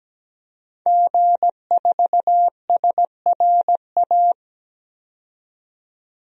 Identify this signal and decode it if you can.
Morse code, heard as a keyed tone, about 17 wpm, 700 Hz. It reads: G4SRA
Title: DXCC-100 Call Signs - 5 Characters - 17wpm